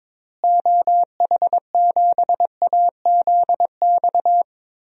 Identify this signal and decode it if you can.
Morse code, heard as a keyed tone, about 22 wpm, 700 Hz. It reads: OH7AZX
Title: DXCC-100 Call Signs - 6 Characters - Repeated 2x - 22wpm